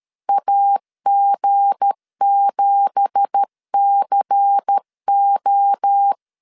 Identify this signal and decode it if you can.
Morse code, heard as a keyed tone, about 13 wpm, 800 Hz. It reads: AG7CO
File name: CW / DET / 1 9 1500C S C 12wpm 3811